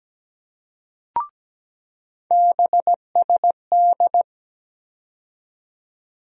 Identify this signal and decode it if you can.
Morse code, heard as a keyed tone, about 17 wpm, 700 Hz. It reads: BSD
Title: USA Call Sign Postfixes - 3 Letters 17wpm